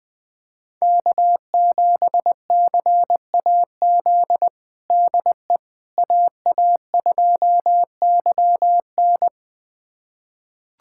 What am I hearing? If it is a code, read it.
Morse code, heard as a keyed tone, about 20 wpm, 700 Hz. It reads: K7CAZ DE AA2YN
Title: USA Callsign DE Callsign - 5 Characters - 20wpm